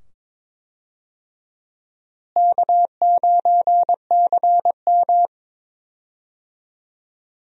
Morse code, heard as a keyed tone, about 22 wpm, 700 Hz. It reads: K9CM
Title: USA Call Signs - 4 Characters - Repeated 2x - 22wpm